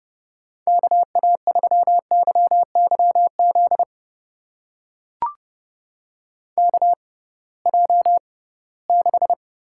Morse code, heard as a keyed tone, about 31 wpm, spaced as for 20 wpm, 700 Hz. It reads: KA3YYZ KJ6
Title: USA Call Signs - 6 Characters - 6x Character Spacing - 30wpm